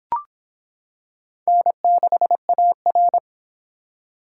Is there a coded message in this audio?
Yes: Morse code from 1.5 to 3.2 s, heard as a keyed tone, about 26 wpm, 700 Hz. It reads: N6AR